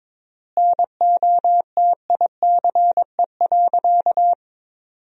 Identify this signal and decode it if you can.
Morse code, heard as a keyed tone, about 22 wpm, 700 Hz. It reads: NOTICE.
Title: 3x Word-Spacing - Sentences from Top 400 Words - 22wpm